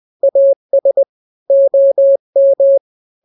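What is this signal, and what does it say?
Morse code, heard as a keyed tone, about 20 wpm, 550 Hz. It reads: AS OM
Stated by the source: Morse drill of two-letter n-grams at 20/18 WPM (Farnsworth), from KK7RR.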